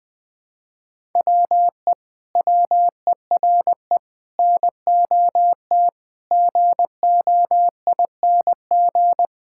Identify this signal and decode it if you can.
Morse code, heard as a keyed tone, about 19 wpm, 700 Hz. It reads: WE WERE NOT GOING
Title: Sentences with 7-Word Phrases - Spoken Beginning - 20wpm